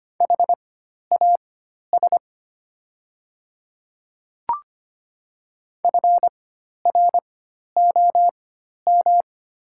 Morse code, heard as a keyed tone, about 25 wpm, spaced as for 12 wpm, 700 Hz. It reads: HAS FROM
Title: 4x Character Spacing - Top 100 Words - 25wpm